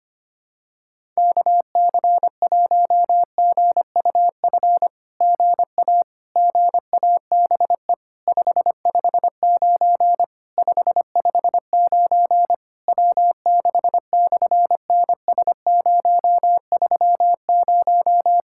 Morse code, heard as a keyed tone, about 24 wpm, 700 Hz. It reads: KC1GUF GA GABE 559 559 W6/NS030
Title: SOTA QSO Practice 25wpm